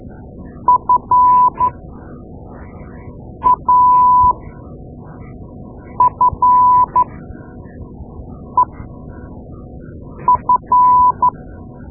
Morse code, heard as a keyed tone, about 12 wpm, spaced as for 4 wpm, 1000 Hz. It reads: FAFEF
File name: Morse Lessons / Morse Lessons / Tape-2A